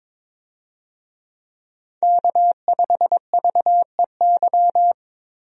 Morse code, heard as a keyed tone, about 22 wpm, 700 Hz. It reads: K5VEY
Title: USA Call Signs - 5 Characters - 22wpm